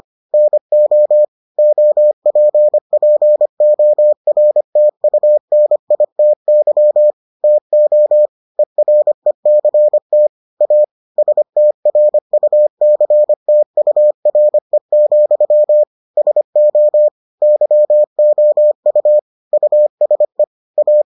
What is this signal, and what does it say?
Morse code, heard as a keyed tone, about 25 wpm, 600 Hz. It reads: NO OPPORTUNITY TO ERECT A STRUCTURE, SO YOU USE A